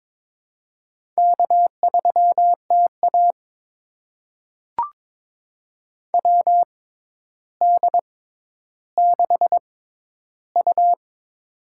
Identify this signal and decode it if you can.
Morse code, heard as a keyed tone, about 22 wpm, spaced as for 11 wpm, 700 Hz. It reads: K3TA WD6U
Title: USA Call Signs - 4 Characters - 6x Character Spacing - 22wpm